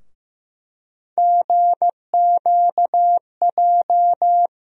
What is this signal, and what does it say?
Morse code, heard as a keyed tone, about 16 wpm, 700 Hz. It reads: GQJ
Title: USA Call Sign Postfixes - 3 Letters 15wpm